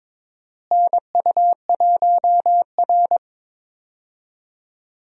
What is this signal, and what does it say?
Morse code, heard as a keyed tone, about 22 wpm, 700 Hz. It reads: NU1R
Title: USA Call Signs - 4 Characters - Repeated 2x - 22wpm